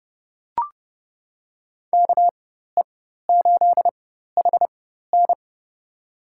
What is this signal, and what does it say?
Morse code, heard as a keyed tone, about 31 wpm, spaced as for 14 wpm, 700 Hz. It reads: KE8HN
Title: USA Call Signs - 4x Character Spacing - 30wpm